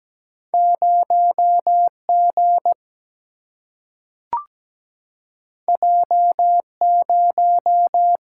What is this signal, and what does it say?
Morse code, heard as a keyed tone, about 17 wpm, 700 Hz. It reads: 0 G J 0
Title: Single-Letter Single-Number 17wpm